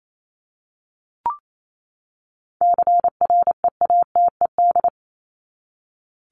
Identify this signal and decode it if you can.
Morse code, heard as a keyed tone, about 28 wpm, 700 Hz. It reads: CREATED
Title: ED Suffix - Top 100 Words with Base Words 28wpm